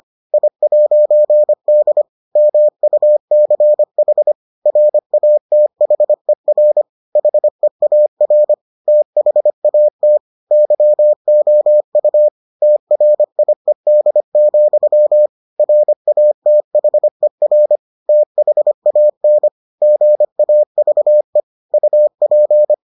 Morse code, heard as a keyed tone, about 25 wpm, 600 Hz. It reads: I'D MUCH RATHER HEAR THAT YOU TRIED, RATHER THAN GAVE UP